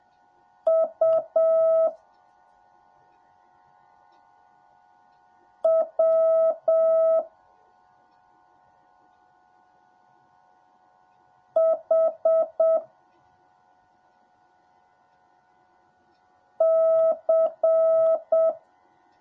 Morse code, heard as a keyed tone, about 7 wpm, 650 Hz. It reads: U W H C
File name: jh8cbh / 2021cw / cw10a